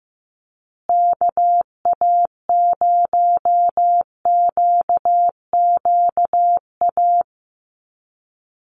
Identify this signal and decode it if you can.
Morse code, heard as a keyed tone, about 15 wpm, 700 Hz. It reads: KA0QQA
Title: USA Call Signs Repeated 3x 15wpm